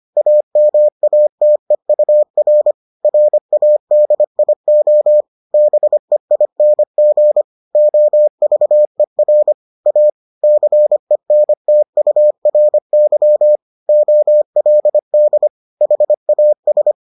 Morse code, heard as a keyed tone, about 25 wpm, 600 Hz. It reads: AMATEUR RADIO BEING OVER A CENTURY OLD HAS